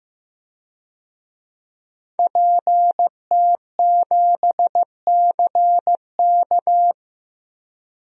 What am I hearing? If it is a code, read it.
Morse code, heard as a keyed tone, about 15 wpm, 700 Hz. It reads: PT7CK
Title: DXCC-100 Call Signs - 5 Characters - Repeated 2x - 15wpm